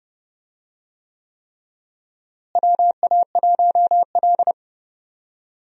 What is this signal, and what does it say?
Morse code, heard as a keyed tone, about 30 wpm, 700 Hz. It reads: WA1L
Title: USA Call Signs - 4 Characters - Repeated 3x - 30wpm